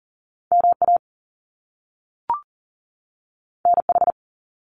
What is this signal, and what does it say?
Morse code, heard as a keyed tone, about 39 wpm, 700 Hz. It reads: MA NH
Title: US States 40wpm